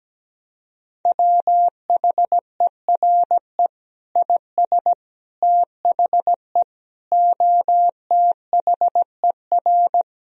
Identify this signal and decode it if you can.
Morse code, heard as a keyed tone, about 17 wpm, 700 Hz. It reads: WHERE IS THE OTHER
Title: Sentences from Top 1000 Words - Review Part 4 - 17wpm